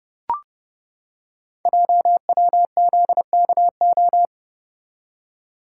Morse code, heard as a keyed tone, about 29 wpm, 700 Hz. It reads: JWZKO
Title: Code Groups - 5 Letters - 30wpm